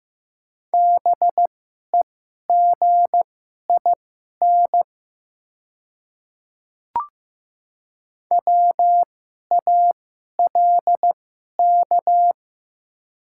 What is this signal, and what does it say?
Morse code, heard as a keyed tone, about 15 wpm, spaced as for 11 wpm, 700 Hz. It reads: BEGIN WALK
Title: 2x Character Spacing - Top 300 Words - 15wpm